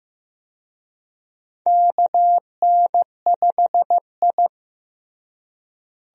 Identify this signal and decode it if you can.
Morse code, heard as a keyed tone, about 15 wpm, 700 Hz. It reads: KN5I
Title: USA Call Signs - 4 Characters - Repeated 2x - 15wpm